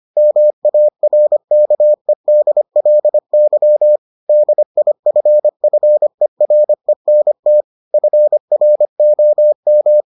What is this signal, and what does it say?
Morse code, heard as a keyed tone, about 25 wpm, 600 Hz. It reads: MARKEDLY DIFFERENT FROM